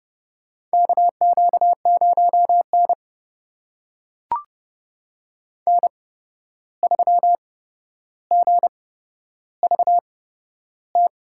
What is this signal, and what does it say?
Morse code, heard as a keyed tone, about 30 wpm, spaced as for 12 wpm, 700 Hz. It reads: KQ0N N3GVT
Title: USA Call Signs - 8x Character Spacing - 30wpm